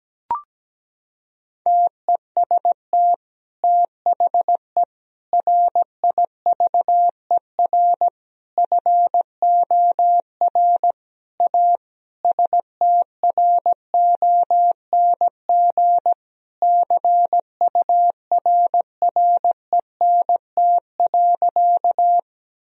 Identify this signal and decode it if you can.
Morse code, heard as a keyed tone, about 17 wpm, 700 Hz. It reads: TEST THE RIVER FOR A STRONG CURRENT.
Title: Sentences from Top 1000 Words - Review Part 4 - 17wpm